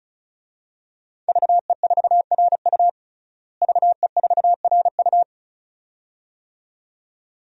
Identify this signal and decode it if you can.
Morse code, heard as a keyed tone, about 34 wpm, 700 Hz. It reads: VE4RU VE4RU
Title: DXCC-100 Call Signs - 5 Characters - Repeated 2x - 35wpm